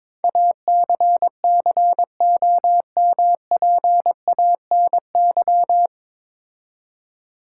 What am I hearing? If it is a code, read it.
Morse code, heard as a keyed tone, about 23 wpm, 700 Hz. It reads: ACCOMPANY
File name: 9 Letter Words - First 3 Letters Given - 22wpm